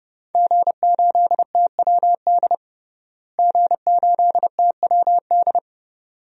Morse code, heard as a keyed tone, about 30 wpm, 700 Hz. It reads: G8TWD G8TWD
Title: DXCC-100 Call Signs - 5 Characters - Repeated 2x - 30wpm